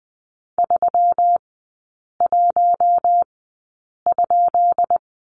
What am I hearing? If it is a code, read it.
Morse code, heard as a keyed tone, about 20 wpm, spaced as for 8 wpm, 690 Hz. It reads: 31?
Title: 133 Lesson 22e - Question Mark - Three Characters 20wpm